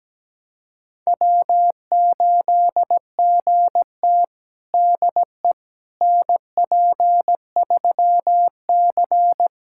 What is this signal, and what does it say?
Morse code, heard as a keyed tone, about 17 wpm, 700 Hz. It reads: W8GT DE NP3C
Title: USA Callsign DE Callsign - 4 Characters - 17wpm